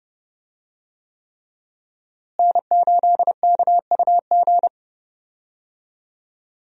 Morse code, heard as a keyed tone, about 31 wpm, 700 Hz. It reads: N8KUG
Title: USA Call Signs - 5 Characters - Repeated 3x - 30wpm